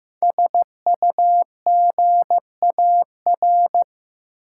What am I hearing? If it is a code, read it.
Morse code, heard as a keyed tone, about 15 wpm, 700 Hz. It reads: SUGAR
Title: Sentences from Top 1200 Words - New Word Drill - 15wpm